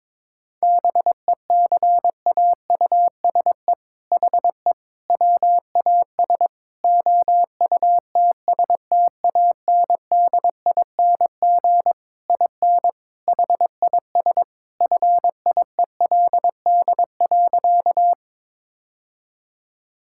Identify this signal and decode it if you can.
Morse code, heard as a keyed tone, about 22 wpm, 700 Hz. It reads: BECAUSE HE WAS OUTSTANDING IN HIS FIELD.
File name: All-Age Appropriate Jokes - 22wpm